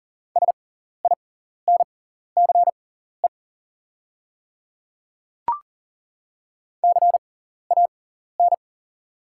Morse code, heard as a keyed tone, about 39 wpm, spaced as for 14 wpm, 700 Hz. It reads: SINCE CAN
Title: 6x Character Spacing - Top 300 Words - 40wpm